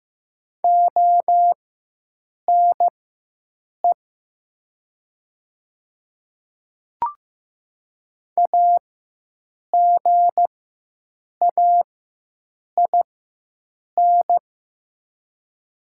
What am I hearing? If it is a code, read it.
Morse code, heard as a keyed tone, about 15 wpm, spaced as for 7 wpm, 700 Hz. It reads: ONE AGAIN
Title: 4x Character Spacing - Top 200 Words - 15wpm